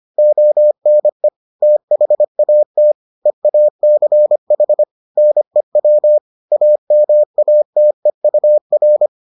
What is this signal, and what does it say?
Morse code, heard as a keyed tone, about 25 wpm, 600 Hz. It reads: ONE THAT EACH NEW AMATEUR